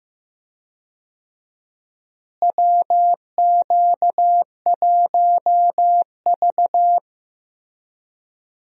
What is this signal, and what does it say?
Morse code, heard as a keyed tone, about 15 wpm, 700 Hz. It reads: WQ1V
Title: USA Call Signs - 4 Characters - Repeated 3x - 15wpm